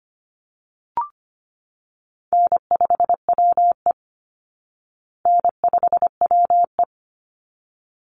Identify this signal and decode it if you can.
Morse code, heard as a keyed tone, about 25 wpm, 700 Hz. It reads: N5WE N5WE
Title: Easy Call Signs Repeated 2x - 25wpm